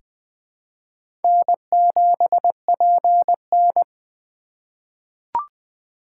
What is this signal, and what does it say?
Morse code, heard as a keyed tone, about 20 wpm, 700 Hz. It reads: N7PN
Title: USA Call Signs - 4 Characters - 3x Character Spacing - 20wpm